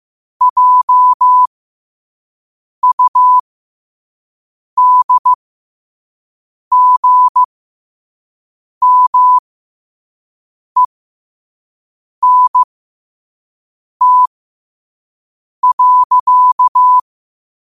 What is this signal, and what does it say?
Morse code, heard as a keyed tone, about 15 wpm, spaced as for 5 wpm, 1000 Hz. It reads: JUDGMENT.
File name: qotdfiles / 5wpm